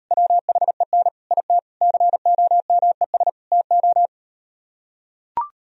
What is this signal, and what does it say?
Morse code, heard as a keyed tone, about 39 wpm, 700 Hz. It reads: WHEN IT COMES TO